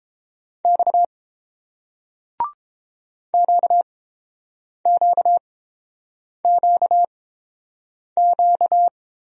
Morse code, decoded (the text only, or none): X QQQQ